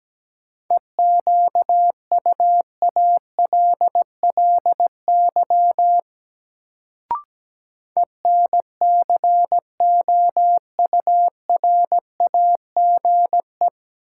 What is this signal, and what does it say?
Morse code, heard as a keyed tone, about 17 wpm, 700 Hz. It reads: EQUALLY ENCOURAGE
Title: Sentences from Top 1400 Words - New Word Drill - 17wpm